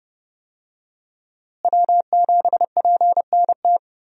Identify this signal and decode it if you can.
Morse code, heard as a keyed tone, about 30 wpm, 700 Hz. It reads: W7PNT
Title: USA Call Signs - 5 Characters - Repeated 2x - 30wpm